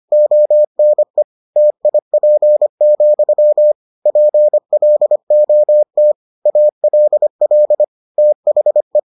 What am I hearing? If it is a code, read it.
Morse code, heard as a keyed tone, about 25 wpm, 600 Hz. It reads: ONE TIP, PLOT ALL THE